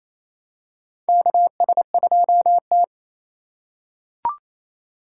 Morse code, heard as a keyed tone, about 27 wpm, 700 Hz. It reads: KS2T
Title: USA Call Signs - 6x Character Spacing - 28wpm